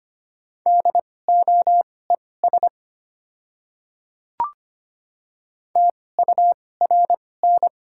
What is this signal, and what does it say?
Morse code, heard as a keyed tone, about 25 wpm, spaced as for 18 wpm, 700 Hz. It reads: DOES TURN
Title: 2x Character Spacing - Top 200 Words - 25wpm